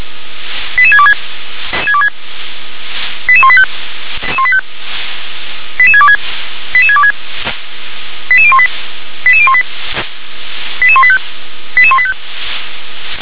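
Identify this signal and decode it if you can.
Five-tone selective call (ZVEI), heard as quick runs of five tones, tones of about 70 ms. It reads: 88626 88185 88626 88626 88018 88018 88185 88185
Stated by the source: Five-tone ZVEI selective call 88625 received on SDR.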